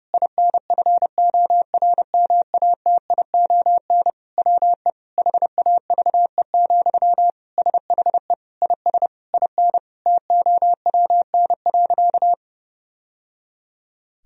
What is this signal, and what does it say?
Morse code, heard as a keyed tone, about 30 wpm, 700 Hz. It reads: INFORMATION WE HAVE, SHE IS IN TOWN.
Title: Sentences from Top 1000 Words - Review Part 4 - 30wpm